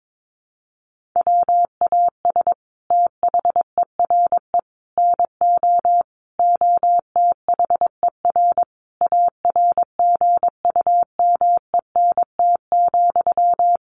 Morse code, heard as a keyed tone, about 22 wpm, 700 Hz. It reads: WAS THERE NO OTHER ARGUMENT,
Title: Sentences from Top 1400 Words 22wpm